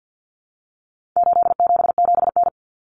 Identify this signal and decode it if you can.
Morse code, heard as a keyed tone, about 50 wpm, 700 Hz. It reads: 877N
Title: Common Cut Numbers - 4-Digit 50wpm